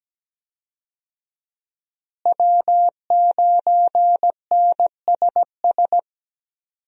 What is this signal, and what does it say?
Morse code, heard as a keyed tone, about 17 wpm, 700 Hz. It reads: W9NSS
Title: USA Call Signs - 5 Characters - Repeated 3x - 17wpm